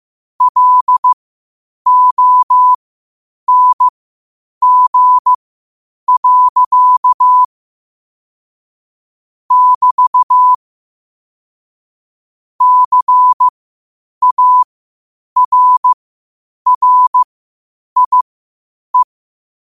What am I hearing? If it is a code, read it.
Morse code, heard as a keyed tone, about 15 wpm, spaced as for 8 wpm, 1000 Hz. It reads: LONG. <BT> CARRIE
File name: qotdfiles / 8wpm